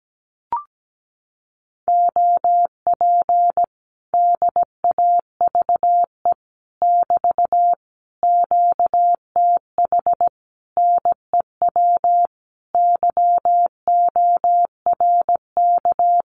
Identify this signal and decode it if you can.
Morse code, heard as a keyed tone, about 17 wpm, 700 Hz. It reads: OP DAVE <BT> QTH NEW YORK
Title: Common QSO Phrases Encore 17wpm